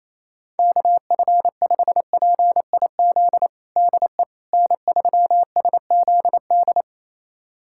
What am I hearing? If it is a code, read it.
Morse code, heard as a keyed tone, about 28 wpm, 700 Hz. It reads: KF5PIZ DE N3SZD
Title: USA Callsign DE Callsign 28wpm